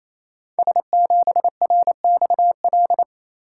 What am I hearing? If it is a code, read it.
Morse code, heard as a keyed tone, about 27 wpm, 700 Hz. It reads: S7RXL
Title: Code Groups - 5 Letter:Numbers - 28wpm